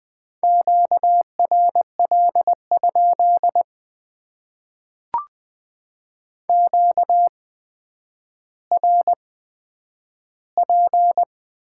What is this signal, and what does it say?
Morse code, heard as a keyed tone, about 20 wpm, spaced as for 10 wpm, 700 Hz. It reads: QRL? QRP
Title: Top 90 QSO Elements - 8x Character Spacing - 20wpm